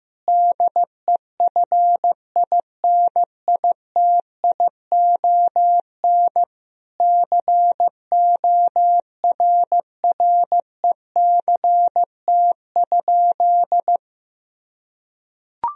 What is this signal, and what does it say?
Morse code, heard as a keyed tone, about 16 wpm, 700 Hz. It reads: DEFINITION CORRECT?
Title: Sentences from Top 1000 Words - Review Part 4 - 15wpm